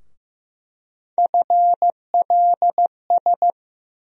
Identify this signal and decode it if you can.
Morse code, heard as a keyed tone, about 15 wpm, 700 Hz. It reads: FLS
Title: USA Call Sign Postfixes - 3 Letters 15wpm